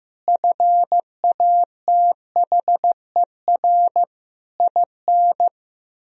Morse code, heard as a keyed tone, about 15 wpm, 700 Hz. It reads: FATHER IN
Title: Sentences from Top 1000 Words - Review Part 4 - 15wpm